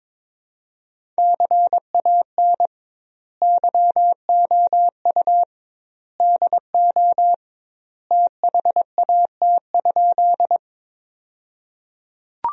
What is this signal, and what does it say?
Morse code, heard as a keyed tone, about 22 wpm, 700 Hz. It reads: CAN YOU DO THAT?
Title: 2x Word-Spacing - Sentences from Top 100 Words - 22wpm